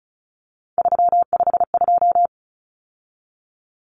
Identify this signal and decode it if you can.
Morse code, heard as a keyed tone, about 34 wpm, 700 Hz. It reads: 352
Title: Numbers - 3-Digit 35wpm